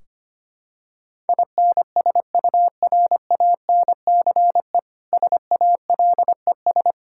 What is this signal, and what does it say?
Morse code, heard as a keyed tone, about 25 wpm, 700 Hz. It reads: INSURANCE SALES
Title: Practice QSO - Exchange 3 - More Info - 25wpm